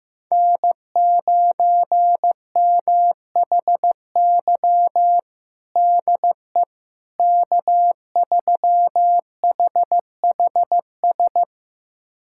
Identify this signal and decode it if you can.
Morse code, heard as a keyed tone, about 15 wpm, 700 Hz. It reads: N9MHY DE K3HHS
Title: USA Callsign DE Callsign - 5 Characters - 15wpm